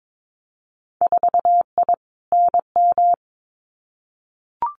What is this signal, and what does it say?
Morse code, heard as a keyed tone, about 22 wpm, 700 Hz. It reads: 4I NM
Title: Winter Field Day Exchanges Encore 22wpm